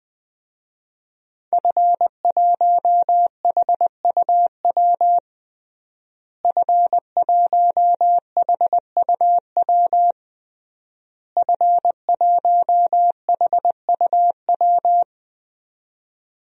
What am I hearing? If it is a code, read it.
Morse code, heard as a keyed tone, about 20 wpm, 700 Hz. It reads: F1HUW F1HUW F1HUW
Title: DXCC-100 Call Signs - 5 Characters - Repeated 3x - 20wpm